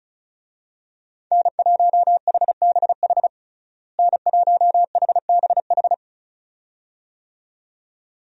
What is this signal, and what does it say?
Morse code, heard as a keyed tone, about 34 wpm, 700 Hz. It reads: N1HBH N1HBH
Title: USA Call Signs - 5 Characters - Repeated 2x - 35wpm